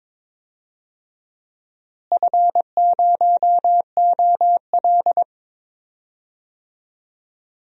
Morse code, heard as a keyed tone, about 22 wpm, 700 Hz. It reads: F0OL
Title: DXCC-100 Call Signs - 4 Characters - Repeated 2x - 22wpm